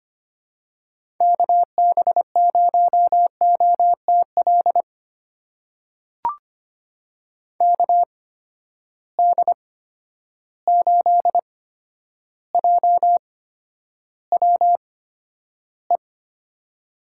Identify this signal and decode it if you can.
Morse code, heard as a keyed tone, about 25 wpm, spaced as for 11 wpm, 700 Hz. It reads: KB0OTL KD8JWE
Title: USA Call Signs - 6 Characters - 8x Character Spacing - 25wpm